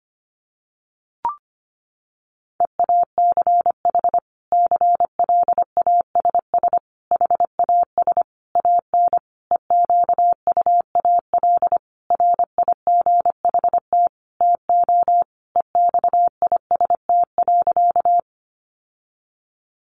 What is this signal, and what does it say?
Morse code, heard as a keyed tone, about 25 wpm, 700 Hz. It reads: EACH CLASS HAS AN EQUAL RIGHT TO EXIST.
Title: Sentences from Top 1400 Words 25wpm